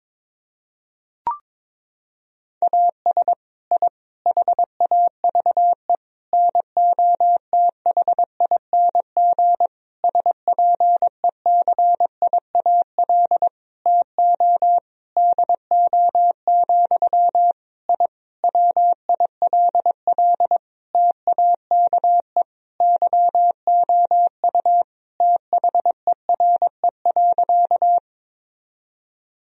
Morse code, heard as a keyed tone, about 22 wpm, 700 Hz. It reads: AS I HAVE NOTHING SPECIAL TO DO, I WILL TAKE YOU THERE.
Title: Sentences from Top 1000 Words - Review Part 3 - 22wpm